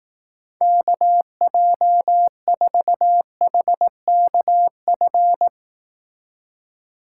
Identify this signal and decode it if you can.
Morse code, heard as a keyed tone, about 18 wpm, 700 Hz. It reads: KJ4HKF
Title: Speed Racing - USA 2x3 Call Signs 15wpm